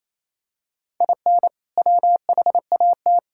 Morse code, heard as a keyed tone, about 28 wpm, 700 Hz. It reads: IN WHAT
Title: Sentences from Top 1000 Words - Review Part 5 - 28wpm